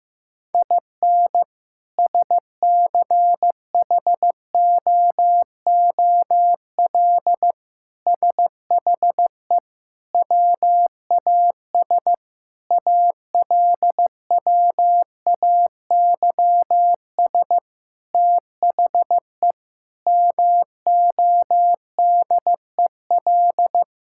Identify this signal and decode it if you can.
Morse code, heard as a keyed tone, about 16 wpm, 700 Hz. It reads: IN SCHOOL SHE WAS ALWAYS THE MODEL
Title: Sentences from Top 1000 Words - Review Part 5 - 15wpm